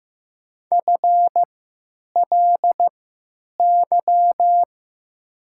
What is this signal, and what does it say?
Morse code, heard as a keyed tone, about 15 wpm, spaced as for 9 wpm, 700 Hz. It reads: FLY
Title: 3x Character Spacing - Top 500 Words - 15wpm